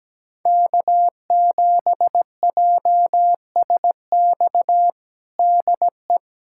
Morse code, heard as a keyed tone, about 17 wpm, 700 Hz. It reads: K7JSX DE
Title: USA Callsign DE Callsign - 5 Characters - 17wpm